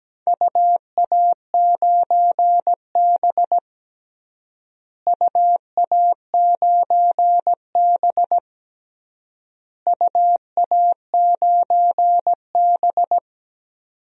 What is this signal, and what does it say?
Morse code, heard as a keyed tone, about 17 wpm, 700 Hz. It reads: UA9B UA9B UA9B
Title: DXCC-100 Call Signs - 4 Characters - Repeated 3x - 17wpm